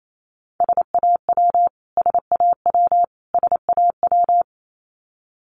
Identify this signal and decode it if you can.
Morse code, heard as a keyed tone, about 28 wpm, 700 Hz. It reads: SAW SAW SAW
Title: Top 300 Words Repeated 28wpm